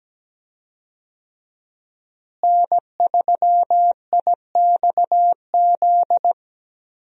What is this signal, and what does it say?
Morse code, heard as a keyed tone, about 17 wpm, 700 Hz. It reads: N3IXZ
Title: USA Call Signs - 5 Characters - Repeated 2x - 17wpm